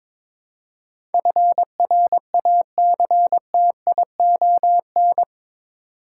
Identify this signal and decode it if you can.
Morse code, heard as a keyed tone, about 21 wpm, 700 Hz. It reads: FRACTION
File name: practice / 22wpm / Eight Letter Words - 22wpm